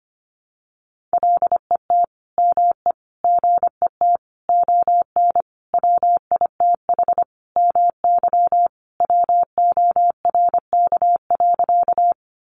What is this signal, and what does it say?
Morse code, heard as a keyed tone, about 25 wpm, 700 Hz. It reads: LET ME GET ON WITH MY WORK.
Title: Sentences from Top 300 Words 25wpm